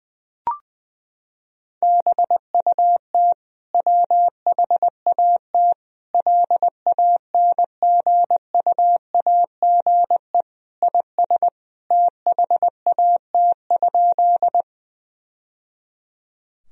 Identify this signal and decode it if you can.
Morse code, heard as a keyed tone, about 20 wpm, 700 Hz. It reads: BUT WHAT LANGUAGE IS THAT?
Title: Sentences from Top 1000 Words - Review Part 3 - 20wpm